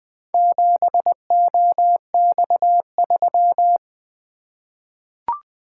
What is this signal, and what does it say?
Morse code, heard as a keyed tone, about 21 wpm, 700 Hz. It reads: 7OX3
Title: Four Letter-Number Combinations - 20wpm